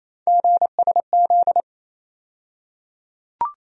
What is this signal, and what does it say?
Morse code, heard as a keyed tone, about 28 wpm, 700 Hz. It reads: GSZ
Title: USA Call Sign Postfixes - 3 Letters 28wpm